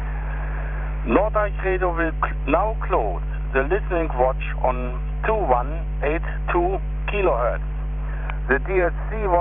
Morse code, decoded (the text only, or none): none